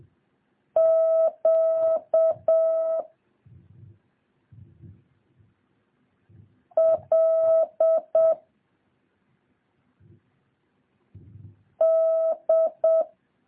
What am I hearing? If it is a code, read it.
Morse code, heard as a keyed tone, about 7 wpm, 650 Hz. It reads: Q L D
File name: jh8cbh / 2023cw